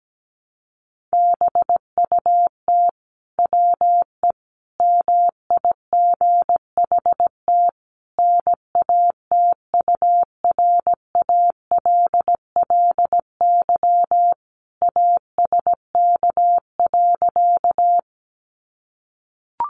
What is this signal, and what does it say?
Morse code, heard as a keyed tone, about 17 wpm, 700 Hz. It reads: BUT WE MIGHT NATURALLY ASK.
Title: Sentences from Top 1300 Words 17wpm